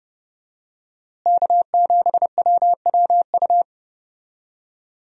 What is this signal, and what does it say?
Morse code, heard as a keyed tone, about 30 wpm, 700 Hz. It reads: K7WWU
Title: USA Call Signs - 5 Characters - Repeated 2x - 30wpm